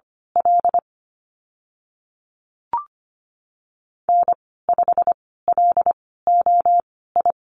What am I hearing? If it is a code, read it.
Morse code, heard as a keyed tone, about 25 wpm, spaced as for 16 wpm, 690 Hz. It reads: L N5LOI
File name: Lesson 15g - Number 5 - Call Signs 25wpm